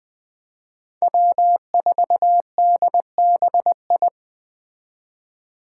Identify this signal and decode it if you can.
Morse code, heard as a keyed tone, about 20 wpm, 700 Hz. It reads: W4DBI
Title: Five Challenging Letter-Number Combinations 20wpm